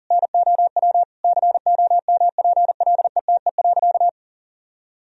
Morse code, heard as a keyed tone, about 40 wpm, 700 Hz. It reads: NOW COMPLETE.
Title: Sentences from Top 400 Words and Abbreviations 40wpm